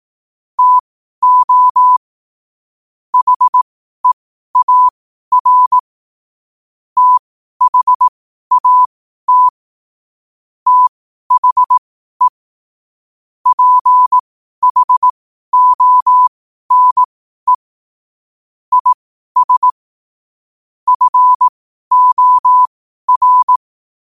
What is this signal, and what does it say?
Morse code, heard as a keyed tone, about 18 wpm, spaced as for 12 wpm, 1000 Hz. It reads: TO HEAR THAT THE PHONE IS FOR
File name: 12wpm